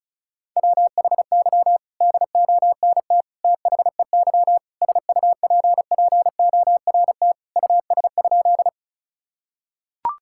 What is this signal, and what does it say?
Morse code, heard as a keyed tone, about 34 wpm, 700 Hz. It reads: WHY DONT THEY SUPPORT US?
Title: Sentences from Top 1000 Words - Review Part 5 - 35wpm